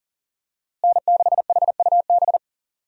Morse code, heard as a keyed tone, about 40 wpm, 700 Hz. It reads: N6HUB
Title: USA Call Signs - 5 Characters - Repeated 3x - 40wpm